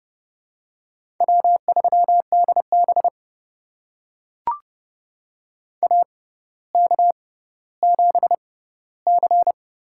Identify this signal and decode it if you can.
Morse code, heard as a keyed tone, about 31 wpm, spaced as for 15 wpm, 700 Hz. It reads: W3DB AK7C
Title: USA Call Signs - 4 Characters - 6x Character Spacing - 30wpm